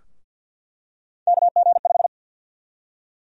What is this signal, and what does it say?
Morse code, heard as a keyed tone, about 53 wpm, 700 Hz. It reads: KG5
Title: USA Call Sign Prefixes - 50wpm